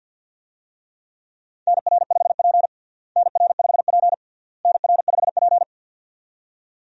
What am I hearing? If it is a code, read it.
Morse code, heard as a keyed tone, about 50 wpm, 700 Hz. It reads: NR5P NR5P NR5P
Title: USA Call Signs - 4 Characters - Repeated 3x - 50wpm